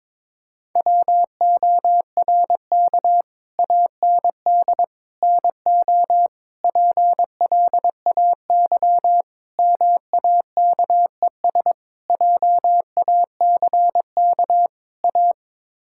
Morse code, heard as a keyed tone, about 22 wpm, 700 Hz. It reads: WORK AND NO PLAY MAKES JACK A
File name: practice / 22wpm / Sets of 7 Words - 22wpm